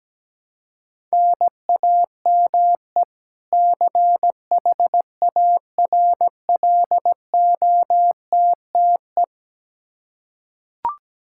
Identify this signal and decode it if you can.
Morse code, heard as a keyed tone, about 17 wpm, 700 Hz. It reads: NAME CHARLOTTE
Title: Practice QSO - Exchange 1 - Beginning - 17wpm